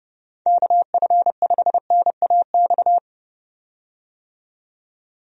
Morse code, heard as a keyed tone, about 32 wpm, 700 Hz. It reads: KF5NAX
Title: USA Call Signs - 6 Characters - 30wpm